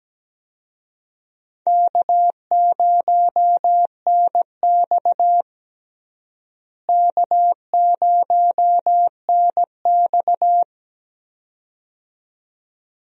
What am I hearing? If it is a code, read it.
Morse code, heard as a keyed tone, about 17 wpm, 700 Hz. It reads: K0NX K0NX
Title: USA Call Signs - 4 Characters - Repeated 2x - 17wpm